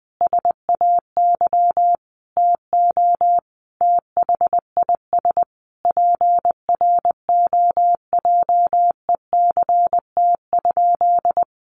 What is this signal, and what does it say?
Morse code, heard as a keyed tone, about 20 wpm, 700 Hz. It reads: SAY TO THIS PROJECT?
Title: Sentences from Top 1400 Words 20wpm